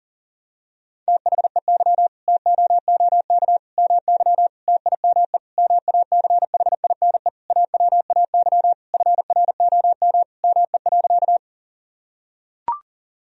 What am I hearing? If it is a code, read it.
Morse code, heard as a keyed tone, about 40 wpm, 700 Hz. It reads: THEY TOOK MY TIME MACHINE AWAY FROM ME.
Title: Sentences from Top 1000 Words - Review Part 2 - 40wpm